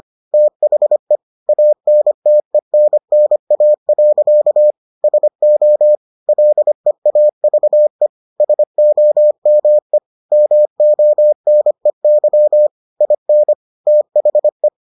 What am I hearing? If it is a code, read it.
Morse code, heard as a keyed tone, about 25 wpm, 600 Hz. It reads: THE ANTENNA. SO LEAVE SOME MONEY IN THE